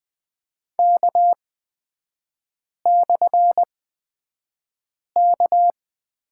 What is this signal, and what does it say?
Morse code, heard as a keyed tone, about 20 wpm, spaced as for 5 wpm, 700 Hz. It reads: K/K
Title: 224 Lesson 36e - BT - Three Characters 20wpm